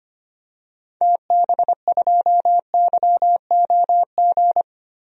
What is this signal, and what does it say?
Morse code, heard as a keyed tone, about 25 wpm, 700 Hz. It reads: TB2YOG
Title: DXCC-100 Call Signs - 6 Characters - Repeated 2x - 25wpm